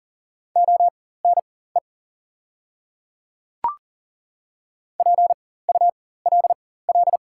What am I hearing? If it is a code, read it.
Morse code, heard as a keyed tone, about 40 wpm, spaced as for 19 wpm, 690 Hz. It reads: ONE PULL
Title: 4x Character Spacing - Top 500 Words - 40wpm